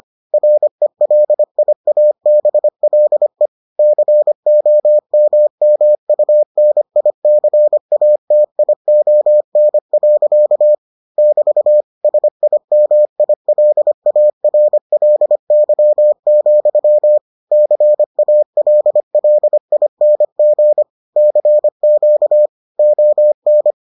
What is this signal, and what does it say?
Morse code, heard as a keyed tone, about 25 wpm, 600 Hz. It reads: RELIABLE COMMUNICATION. <BT> SIMILARLY, CALLING CQ ON